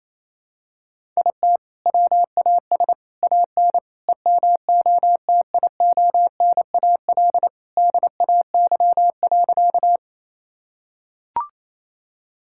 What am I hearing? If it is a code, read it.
Morse code, heard as a keyed tone, about 28 wpm, 700 Hz. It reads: IT WAS AN EMOTIONAL DAY.
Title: Sentences from Top 1000 Words - Review Part 5 - 28wpm